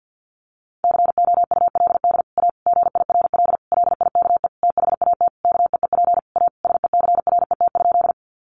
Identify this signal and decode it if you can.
Morse code, heard as a keyed tone, about 50 wpm, 700 Hz. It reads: COULD A GIRL LIKE THAT KEEP A SECRET?
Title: Sentences from Top 1200 Words 50wpm